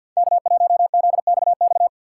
Morse code, heard as a keyed tone, about 48 wpm, 700 Hz. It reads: K1ZXX
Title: USA Call Signs - 6x Character Spacing - 50wpm